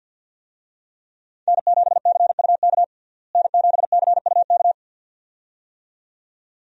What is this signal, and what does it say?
Morse code, heard as a keyed tone, about 48 wpm, 700 Hz. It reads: N7CUK N7CUK
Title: USA Call Signs - 5 Characters - Repeated 2x - 50wpm